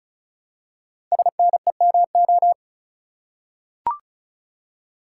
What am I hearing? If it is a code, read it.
Morse code, heard as a keyed tone, about 36 wpm, 700 Hz. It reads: SNEMO
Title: Code Groups - 5 Letters - 35wpm